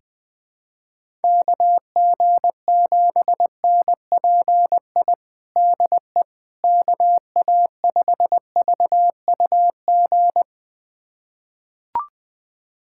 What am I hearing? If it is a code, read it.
Morse code, heard as a keyed tone, about 21 wpm, 700 Hz. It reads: KG7NPI DE KA5VUG
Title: USA Callsign DE Callsign - 6 Characters - 20wpm